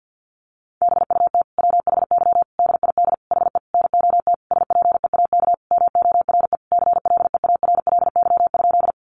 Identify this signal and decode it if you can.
Morse code, heard as a keyed tone, about 50 wpm, 700 Hz. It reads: BUT WHY DID HE NOT SPEAK MORE CLEARLY?
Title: Sentences from Top 1300 Words 50wpm